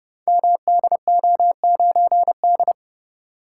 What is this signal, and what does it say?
Morse code, heard as a keyed tone, about 30 wpm, 700 Hz. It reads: MDO9D
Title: Code Groups - 5 Letter:Numbers - 30wpm